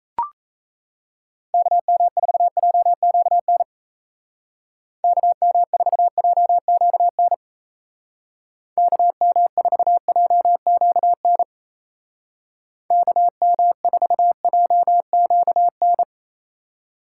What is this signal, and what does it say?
Morse code, heard as a keyed tone, about 35 wpm, 700 Hz. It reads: KM4JQN KM4JQN KM4JQN KM4JQN
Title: Speed Racing - USA 2x3 Call Signs 28wpm